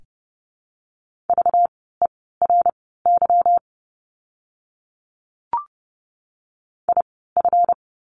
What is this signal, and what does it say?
Morse code, heard as a keyed tone, about 30 wpm, spaced as for 17 wpm, 700 Hz. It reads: VERY IF
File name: course / 30:10wpm / 237 Lesson 38f - Letter q - Words 30wpm